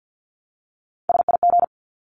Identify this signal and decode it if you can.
Morse code, heard as a keyed tone, about 50 wpm, 700 Hz. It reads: SIG
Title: Abbreviations 50wpm